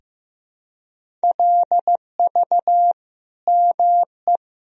Morse code, heard as a keyed tone, about 15 wpm, 700 Hz. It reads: LV ME
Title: Sentences from Top 400 Words and Abbreviations 15wpm